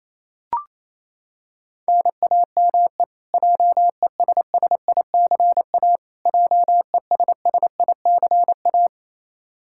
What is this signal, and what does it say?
Morse code, heard as a keyed tone, about 28 wpm, 700 Hz. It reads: NAME JESSICA JESSICA
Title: Practice QSO - Exchange 1 - Beginning - 28wpm